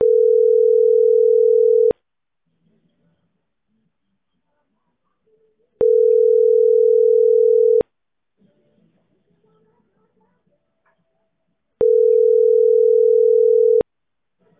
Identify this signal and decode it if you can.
Ringback tone (North America), 440 + 480 Hz.